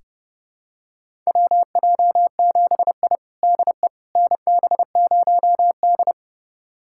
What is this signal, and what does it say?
Morse code, heard as a keyed tone, about 30 wpm, 700 Hz. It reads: WJ7I DE NB0D
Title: USA Callsign DE Callsign - 4 Characters - 30wpm